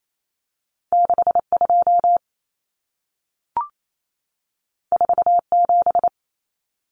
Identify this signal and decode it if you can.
Morse code, heard as a keyed tone, about 28 wpm, 700 Hz. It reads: 62 47
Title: Numbers - 2-Digit 28wpm